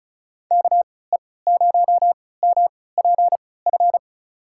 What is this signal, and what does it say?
Morse code, heard as a keyed tone, about 36 wpm, spaced as for 20 wpm, 700 Hz. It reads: KE0MPF
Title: USA Call Signs - 6 Characters - 3x Character Spacing - 35wpm